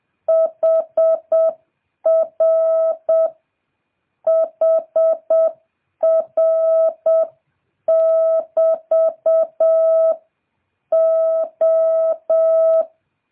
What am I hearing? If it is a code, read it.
Morse code, heard as a keyed tone, about 7 wpm, 650 Hz. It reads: HR HR <BT> O